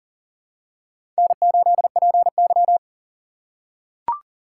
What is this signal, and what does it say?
Morse code, heard as a keyed tone, about 40 wpm, 700 Hz. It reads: N8PY
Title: USA Call Signs - 4 Characters - 4x Character Spacing - 40wpm